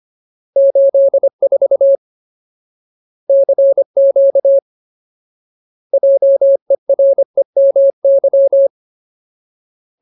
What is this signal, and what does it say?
Morse code, heard as a keyed tone, about 25 wpm, 550 Hz. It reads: 84 CQ JEREMY
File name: Ultimate Word List / 25WPM Ultimate